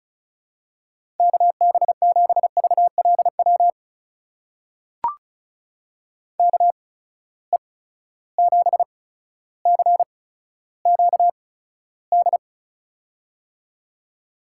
Morse code, heard as a keyed tone, about 36 wpm, spaced as for 15 wpm, 700 Hz. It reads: KB7VLW KE7CQD
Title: USA Call Signs - 6 Characters - 8x Character Spacing - 35wpm